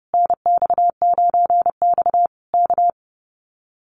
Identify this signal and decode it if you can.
Morse code, heard as a keyed tone, about 30 wpm, 700 Hz. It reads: NX9X K
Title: Common QSO Phrases Encore 30wpm